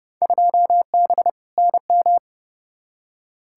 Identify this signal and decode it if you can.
Morse code, heard as a keyed tone, about 32 wpm, 700 Hz. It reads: 2B NM
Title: Field Day Exchanges Encore - 30wpm